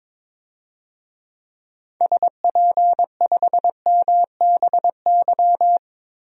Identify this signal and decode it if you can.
Morse code, heard as a keyed tone, about 22 wpm, 700 Hz. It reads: SP5MBY
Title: DXCC-100 Call Signs - 6 Characters - 22wpm